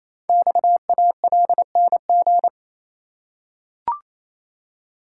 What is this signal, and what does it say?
Morse code, heard as a keyed tone, about 27 wpm, 700 Hz. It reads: XALNG
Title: Code Groups - 5 Letters - 28wpm